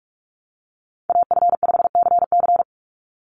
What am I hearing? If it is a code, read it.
Morse code, heard as a keyed tone, about 44 wpm, 700 Hz. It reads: AF5CC
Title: Call Signs Repeated 3x - 45wpm